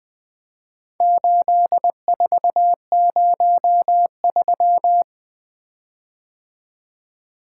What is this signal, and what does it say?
Morse code, heard as a keyed tone, about 20 wpm, 700 Hz. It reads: 8403
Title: Four Letter-Number Combinations - 20wpm